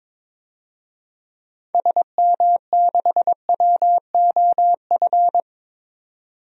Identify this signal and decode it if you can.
Morse code, heard as a keyed tone, about 22 wpm, 700 Hz. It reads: SM6WOF
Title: DXCC-100 Call Signs - 6 Characters - 22wpm